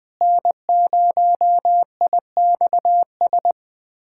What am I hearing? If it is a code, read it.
Morse code, heard as a keyed tone, about 20 wpm, 700 Hz. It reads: N0IXS
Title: USA Callsign DE Callsign - 5 Characters - 20wpm